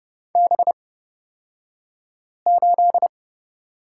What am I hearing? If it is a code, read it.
Morse code, heard as a keyed tone, about 30 wpm, spaced as for 5 wpm, 700 Hz. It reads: B8
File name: Lesson 37d - Letter x - Two Characters 30wpm